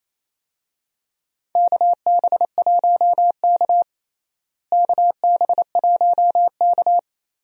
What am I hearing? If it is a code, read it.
Morse code, heard as a keyed tone, about 28 wpm, 700 Hz. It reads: KB1K KB1K
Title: USA Call Signs - 4 Characters - Repeated 3x - 28wpm